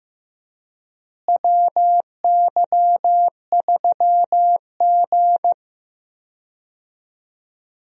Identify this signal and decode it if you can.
Morse code, heard as a keyed tone, about 15 wpm, 700 Hz. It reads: WY3G
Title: USA Call Signs - 4 Characters - Repeated 3x - 15wpm